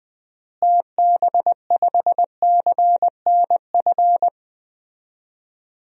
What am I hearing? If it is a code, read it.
Morse code, heard as a keyed tone, about 20 wpm, 700 Hz. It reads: TB5CNF
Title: DXCC-100 Call Signs - 6 Characters - Repeated 2x - 20wpm